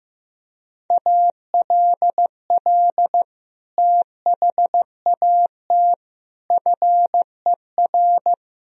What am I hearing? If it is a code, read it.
Morse code, heard as a keyed tone, about 15 wpm, 700 Hz. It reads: ALL THAT FER
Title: Sentences from Top 500 Words and Abbreviations 15wpm